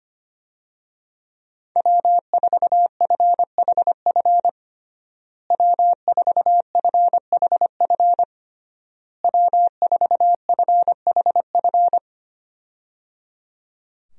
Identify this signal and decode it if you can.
Morse code, heard as a keyed tone, about 25 wpm, 700 Hz. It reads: W4FHF W4FHF W4FHF
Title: USA Call Signs - 5 Characters - Repeated 3x - 25wpm